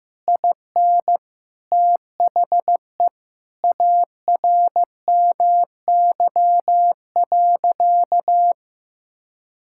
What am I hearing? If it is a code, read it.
Morse code, heard as a keyed tone, about 16 wpm, 700 Hz. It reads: IN THE ARMY.
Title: Sentences from Top 1000 Words - Review Part 6 - 15wpm